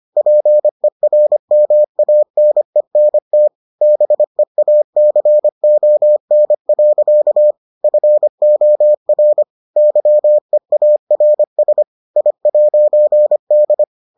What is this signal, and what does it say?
Morse code, heard as a keyed tone, about 25 wpm, 600 Hz. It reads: PERMANENT BEACON. FOR YEARS I'D